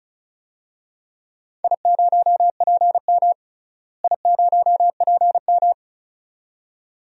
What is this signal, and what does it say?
Morse code, heard as a keyed tone, about 36 wpm, 700 Hz. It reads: I0PM I0PM
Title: DXCC-100 Call Signs - 4 Characters - Repeated 2x - 35wpm